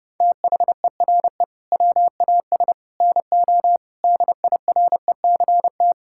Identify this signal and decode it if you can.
Morse code, heard as a keyed tone, about 30 wpm, 700 Hz. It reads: THERE WAS NO DIRECT
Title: Sentences from Top 1000 Words - Review Part 6 - 30wpm